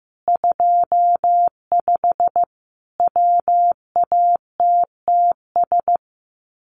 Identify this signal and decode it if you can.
Morse code, heard as a keyed tone, about 15 wpm, 700 Hz. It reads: 25 WATTS
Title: Common QSO Phrases 15wpm